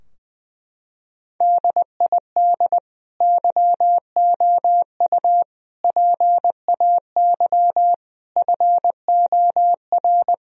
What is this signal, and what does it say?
Morse code, heard as a keyed tone, about 20 wpm, 700 Hz. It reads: DID YOU PAY FOR
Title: Sentences from Top 1000 Words - Review Part 3 - 20wpm